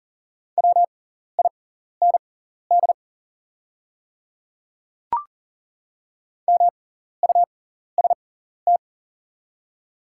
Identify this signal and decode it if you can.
Morse code, heard as a keyed tone, about 40 wpm, spaced as for 14 wpm, 700 Hz. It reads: WIND MUST
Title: 6x Character Spacing - Top 500 Words - 40wpm